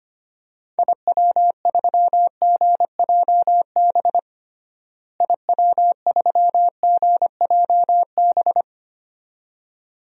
Morse code, heard as a keyed tone, about 25 wpm, 700 Hz. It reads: IW3GJB IW3GJB
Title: DXCC-100 Call Signs - 6 Characters - Repeated 2x - 25wpm